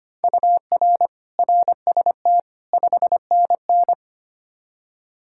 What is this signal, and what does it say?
Morse code, heard as a keyed tone, about 25 wpm, 700 Hz. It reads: UR RST 5NN
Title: Practice QSO - Exchange 1 - Beginning - 25wpm